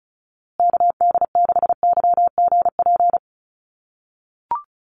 Morse code, heard as a keyed tone, about 34 wpm, 700 Hz. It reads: KD6YGP
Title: USA Call Signs Repeated 3x 35wpm